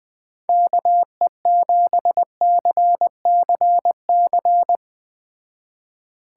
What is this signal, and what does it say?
Morse code, heard as a keyed tone, about 20 wpm, 700 Hz. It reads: KE7CCC
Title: Speed Racing - USA 2x3 Call Signs 17wpm